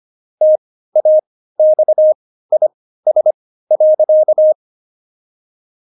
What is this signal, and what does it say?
Morse code, heard as a keyed tone, about 25 wpm, spaced as for 15 wpm, 620 Hz. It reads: TAXIS.
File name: Jokes-25-char-15-eff